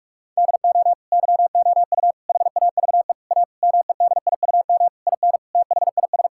Morse code, heard as a keyed tone, about 41 wpm, 690 Hz. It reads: DO YOU HAVE A MEDIUM IN THIS
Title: Sentences from Top 1000 Words - Review Part 6 - 45wpm